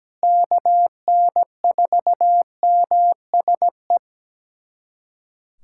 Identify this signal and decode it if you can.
Morse code, heard as a keyed tone, about 17 wpm, 700 Hz. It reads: KN4MSE
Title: USA Callsign DE Callsign 17wpm